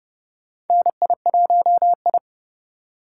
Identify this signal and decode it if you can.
Morse code, heard as a keyed tone, about 32 wpm, 700 Hz. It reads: NI1I
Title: USA Call Signs - 4 Characters - 4x Character Spacing - 30wpm